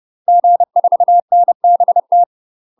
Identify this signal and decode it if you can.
Morse code, heard as a keyed tone, about 30 wpm, 700 Hz. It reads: G4NBT
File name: UK MASTER QSOs TRAINING / 30WPM / UK MASTER QSOs 0014